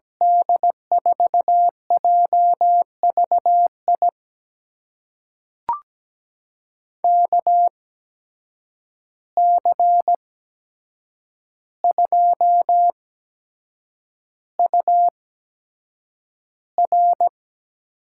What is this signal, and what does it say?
Morse code, heard as a keyed tone, about 17 wpm, spaced as for 7 wpm, 700 Hz. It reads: D4JVI KC2UR
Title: USA Call Signs - 6 Characters - 8x Character Spacing - 17wpm